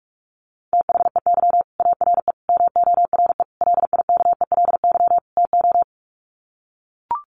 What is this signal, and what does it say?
Morse code, heard as a keyed tone, about 44 wpm, 700 Hz. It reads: THEY ARE MORE LIKELY TO
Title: Sets of 5 Words Encore2 45wpm